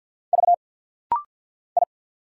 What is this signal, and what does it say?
Morse code, heard as a keyed tone, about 50 wpm, 700 Hz. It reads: V I
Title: Single Letters - Rapid-Fire 50wpm